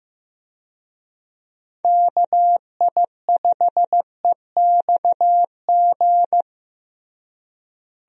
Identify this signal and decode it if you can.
Morse code, heard as a keyed tone, about 15 wpm, 700 Hz. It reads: KI5EXG
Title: USA Call Signs - 6 Characters - Repeated 2x - 15wpm